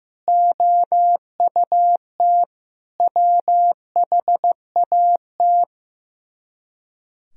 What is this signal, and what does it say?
Morse code, heard as a keyed tone, about 15 wpm, 700 Hz. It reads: OUT WHAT
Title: Speed-Racing - Sets of 5 Words 15wpm